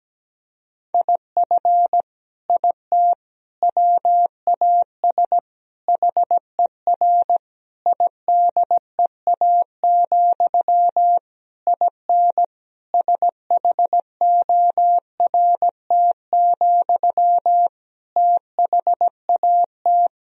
Morse code, heard as a keyed tone, about 17 wpm, 700 Hz. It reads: IF IT WAS HER IDEA, IN SHORT, THAT
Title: Sentences from Top 1000 Words - Review Part 2 - 17wpm